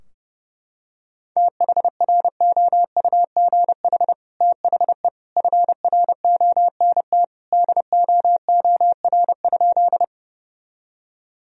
Morse code, heard as a keyed tone, about 32 wpm, 700 Hz. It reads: THROUGH THE FRONT DOOR?
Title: Sentences from Top 1000 Words - Review Part 2 - 30wpm